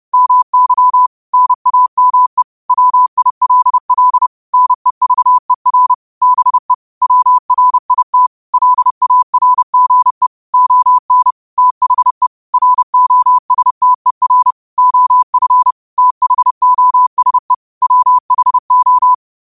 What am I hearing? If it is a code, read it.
Morse code, heard as a keyed tone, about 30 wpm, 1000 Hz. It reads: MY NAME WILL NEVER BE WRIT LARGE ON THE ROSTER OF THOSE WHO
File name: qotdfiles / 30wpm